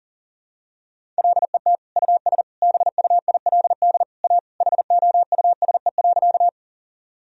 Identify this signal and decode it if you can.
Morse code, heard as a keyed tone, about 37 wpm, 700 Hz. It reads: LET US BUILD A HOUSE.
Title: Sentences from Top 1000 Words - Review Part 1 - 40wpm